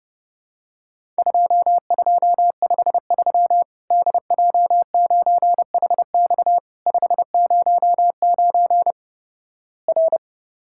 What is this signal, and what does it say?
Morse code, heard as a keyed tone, about 30 wpm, 700 Hz. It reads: 2253 DJ9HX 509 R
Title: Worked All Europe DX Contest - QTC Exchange - 30wpm